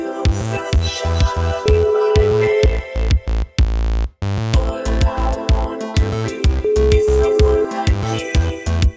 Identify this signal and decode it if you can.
Ringback tone (Europe, 425 Hz), with noise.